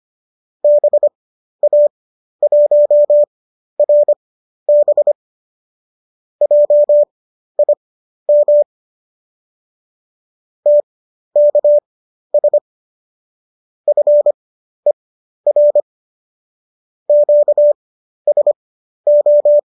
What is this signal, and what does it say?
Morse code, heard as a keyed tone, about 25 wpm, spaced as for 12 wpm, 590 Hz. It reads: BA1RB JIM TKS FER QSO